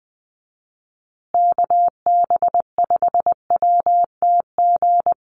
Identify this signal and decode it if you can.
Morse code, heard as a keyed tone, about 20 wpm, 700 Hz. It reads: KB5WTG
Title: USA Call Signs Repeated 3x 20wpm